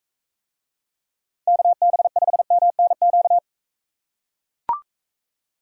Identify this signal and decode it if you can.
Morse code, heard as a keyed tone, about 43 wpm, 700 Hz. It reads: KB5MNQ